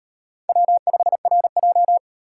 Morse code, heard as a keyed tone, about 35 wpm, 700 Hz. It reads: W5RJ